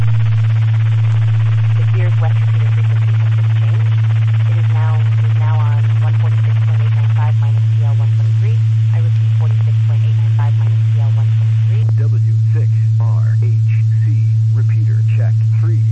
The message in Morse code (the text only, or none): none